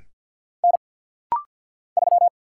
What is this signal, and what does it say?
Morse code, heard as a keyed tone, about 48 wpm, 700 Hz. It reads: N 3
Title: Single Letter-Number - Rapid-Fire 50wpm